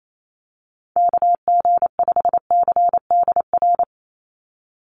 Morse code, heard as a keyed tone, about 28 wpm, 700 Hz. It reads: KG5CDR
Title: USA Call Signs 28wpm